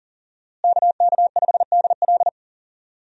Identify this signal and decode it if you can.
Morse code, heard as a keyed tone, about 40 wpm, 700 Hz. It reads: KK5DL
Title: USA Call Signs - 5 Characters - Repeated 3x - 40wpm